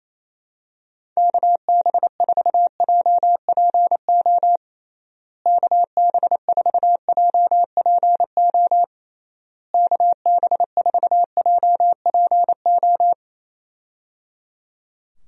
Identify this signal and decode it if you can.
Morse code, heard as a keyed tone, about 28 wpm, 700 Hz. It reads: KB4JPO KB4JPO KB4JPO
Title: USA Call Signs - 6 Characters - Repeated 3x - 28wpm